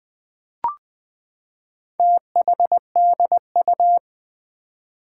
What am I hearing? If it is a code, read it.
Morse code, heard as a keyed tone, about 20 wpm, 700 Hz. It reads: THDU
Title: Four Challenging Letter-Number Combinations 20wpm